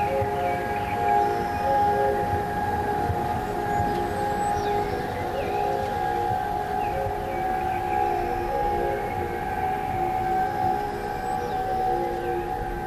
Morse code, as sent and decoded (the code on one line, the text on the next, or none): none